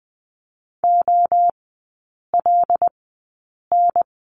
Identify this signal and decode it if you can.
Morse code, heard as a keyed tone, about 20 wpm, spaced as for 8 wpm, 700 Hz. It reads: OLN
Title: 138 Lesson 23e - Letter f - Three Characters 20wpm